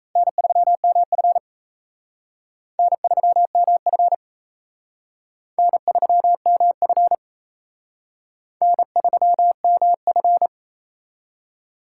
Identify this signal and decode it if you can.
Morse code, heard as a keyed tone, about 34 wpm, 700 Hz. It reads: N3MF N3MF N3MF N3MF
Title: Speed Racing - USA 1x2 Call Signs 28wpm